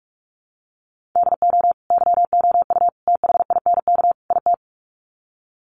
Morse code, heard as a keyed tone, about 44 wpm, 700 Hz. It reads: DO YOU THINK IT
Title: Sets of 4 Words 45wpm